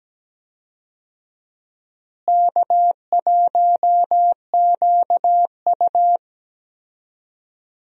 Morse code, heard as a keyed tone, about 17 wpm, 700 Hz. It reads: K1QU
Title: USA Call Signs - 4 Characters - Repeated 3x - 17wpm